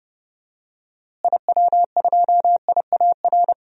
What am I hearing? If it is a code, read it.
Morse code, heard as a keyed tone, about 32 wpm, 700 Hz. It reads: IW2IAR
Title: DXCC-100 Call Signs - 6 Characters - Repeated 2x - 30wpm